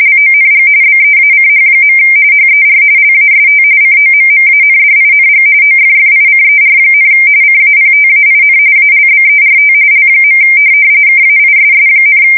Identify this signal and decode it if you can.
RTTY, about 45.45 baud. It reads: LONG BEACH. HER LAST COMMANDING OFFICER, CAPTAIN ALBERT L. KAISS,
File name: fsk170 / USS-Missouri